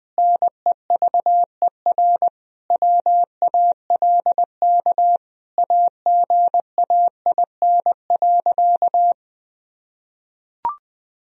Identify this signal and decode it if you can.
Morse code, heard as a keyed tone, about 20 wpm, 700 Hz. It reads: NEVER WALK AGAIN.
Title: Sentences from Top 1000 Words - Review Part 1 - 20wpm